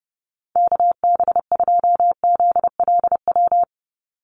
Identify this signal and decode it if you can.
Morse code, heard as a keyed tone, about 30 wpm, 700 Hz. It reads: KB2ZLW
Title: USA Call Signs 30wpm